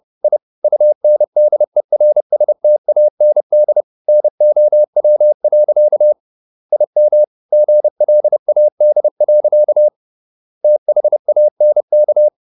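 Morse code, heard as a keyed tone, about 30 wpm, 600 Hz. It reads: I UNDERSTAND NOW. IM GLAD. THANK
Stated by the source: Morse code at 30 wpm, the novel Never by Ken Follett in English, from CW med Gnister.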